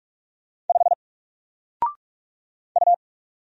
Morse code, heard as a keyed tone, about 46 wpm, 700 Hz. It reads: 5 U
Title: Single Character (Letter:Number:Punctuation:Prosign) 45wpm